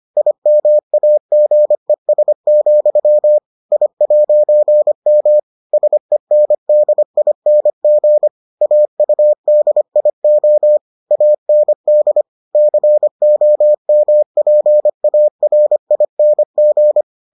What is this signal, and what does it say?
Morse code, heard as a keyed tone, about 25 wpm, 600 Hz. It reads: IMAGES, I'M SENDING AUDIO AND COMPARING